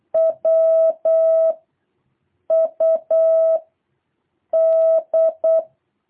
Morse code, heard as a keyed tone, about 8 wpm, 660 Hz. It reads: W U D
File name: jh8cbh / 2023cw